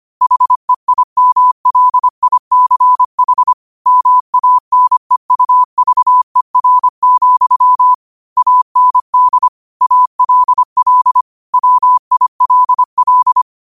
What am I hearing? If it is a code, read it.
Morse code, heard as a keyed tone, about 25 wpm, 1000 Hz. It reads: HEIMLICH MANEUVER, AND ALL WILL
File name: qotdfiles / 25wpm